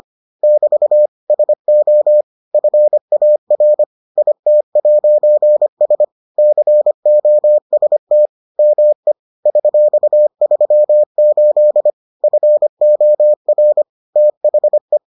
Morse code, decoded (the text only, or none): <BT> SO FAR IT'S COST ME $38 FOR THE